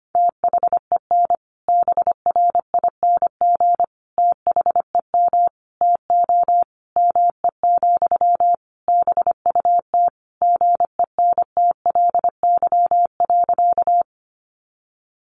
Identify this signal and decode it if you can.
Morse code, heard as a keyed tone, about 25 wpm, 700 Hz. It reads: THEN BRING THEM TO ME, BUT GENTLY.
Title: Sentences from Top 1500 Words 25wpm